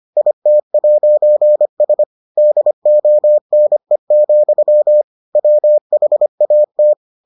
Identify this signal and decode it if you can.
Morse code, heard as a keyed tone, about 25 wpm, 600 Hz. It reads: IT'S DONE, WHAT